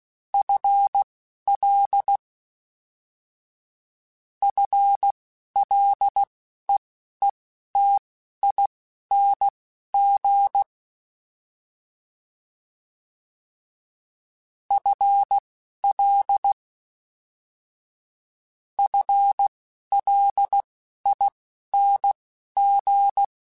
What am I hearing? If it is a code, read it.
Morse code, heard as a keyed tone, about 16 wpm, spaced as for 11 wpm, 800 Hz. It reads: FL FLEETING FL FLING
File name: FL / FL mp3s / FL Prefixes 1x1 @10